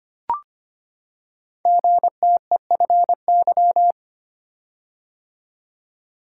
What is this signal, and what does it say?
Morse code, heard as a keyed tone, about 25 wpm, 700 Hz. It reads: GTEFY
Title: Code Groups - 5 Letter:Numbers - 25wpm